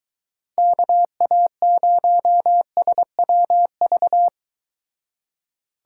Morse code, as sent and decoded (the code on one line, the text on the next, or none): -.- .- ----- ... .-- ...-
KA0SWV